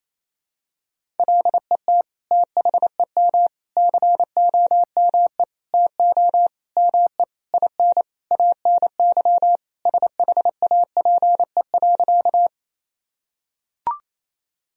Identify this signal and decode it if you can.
Morse code, heard as a keyed tone, about 28 wpm, 700 Hz. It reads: LET THEM COME TO ME IN ANY SHAPE.
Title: Sentences from Top 1000 Words - Review Part 3 - 28wpm